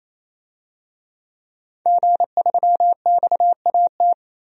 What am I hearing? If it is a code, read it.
Morse code, heard as a keyed tone, about 28 wpm, 700 Hz. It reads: G3XAT
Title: DXCC-100 Call Signs - 5 Characters - 28wpm